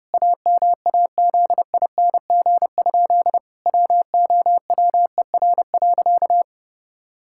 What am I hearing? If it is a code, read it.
Morse code, heard as a keyed tone, about 30 wpm, 700 Hz. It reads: AMAZING? WOWER.
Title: Pangram Challenge v2 - 30wpm